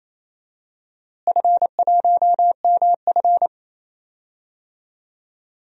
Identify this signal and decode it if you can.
Morse code, heard as a keyed tone, about 27 wpm, 700 Hz. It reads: F1MF
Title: DXCC-100 Call Signs - 4 Characters - Repeated 3x - 28wpm